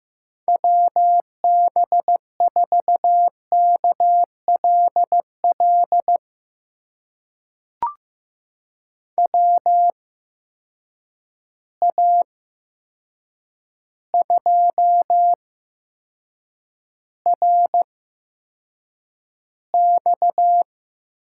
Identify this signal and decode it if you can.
Morse code, heard as a keyed tone, about 15 wpm, spaced as for 7 wpm, 700 Hz. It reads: WB4KLL WA2RX
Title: USA Call Signs - 6 Characters - 8x Character Spacing - 15wpm